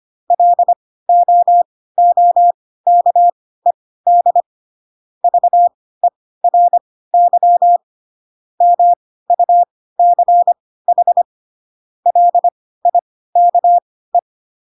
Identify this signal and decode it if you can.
Morse code, heard as a keyed tone, about 25 wpm, spaced as for 16 wpm, 700 Hz. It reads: LOOKED VERY MUCH LIKE